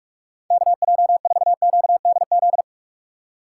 Morse code, heard as a keyed tone, about 46 wpm, 700 Hz. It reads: KJ4QDZ
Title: USA Call Signs - 4x Character Spacing - 45wpm